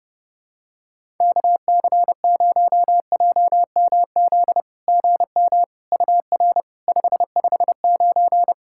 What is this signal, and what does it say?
Morse code, heard as a keyed tone, about 30 wpm, 690 Hz. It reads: KC0JMZ GM UR 559
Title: SOTA QSO Practice 30wpm